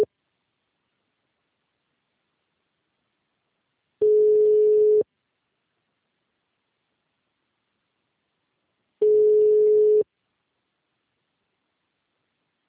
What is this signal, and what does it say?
Ringback tone (Europe, 425 Hz).